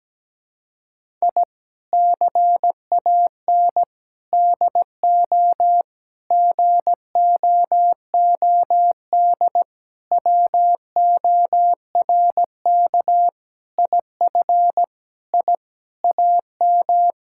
Morse code, heard as a keyed tone, about 17 wpm, 700 Hz. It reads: I CAN DO GOOD WORK IF I AM
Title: Sentences from Top 1000 Words - Review Part 6 - 17wpm